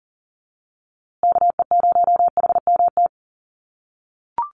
Morse code, heard as a keyed tone, about 40 wpm, 700 Hz. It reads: KE0HMT
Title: USA Call Signs Repeated 2x 40wpm